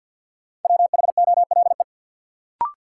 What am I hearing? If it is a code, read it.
Morse code, heard as a keyed tone, about 50 wpm, 700 Hz. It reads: WHOLE
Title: Sentences from Top 400 Words - New Word Drill - 50wpm